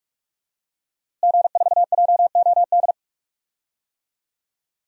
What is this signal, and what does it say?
Morse code, heard as a keyed tone, about 46 wpm, 700 Hz. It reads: G4JOD
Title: DXCC-100 Call Signs - 5 Characters - Repeated 3x - 45wpm